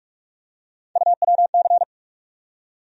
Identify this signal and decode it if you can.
Morse code, heard as a keyed tone, about 45 wpm, 700 Hz. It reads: UWC
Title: USA Call Sign Postfixes - 3 Letters 45wpm